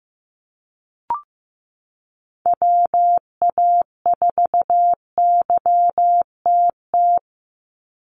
Morse code, heard as a keyed tone, about 15 wpm, 700 Hz. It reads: WA4YTT
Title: USA Call Signs 15wpm